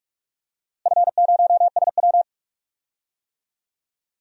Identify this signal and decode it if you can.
Morse code, heard as a keyed tone, about 46 wpm, 700 Hz. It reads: F0SW
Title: DXCC-100 Call Signs - 4 Characters - Repeated 3x - 45wpm